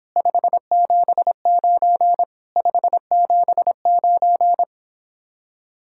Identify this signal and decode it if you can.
Morse code, heard as a keyed tone, about 31 wpm, 700 Hz. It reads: 579 579